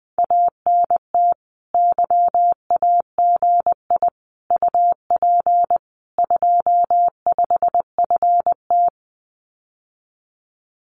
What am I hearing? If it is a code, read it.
Morse code, heard as a keyed tone, about 20 wpm, 700 Hz. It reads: ANT YAGI UP 25FT
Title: Common QSO Phrases Encore 20wpm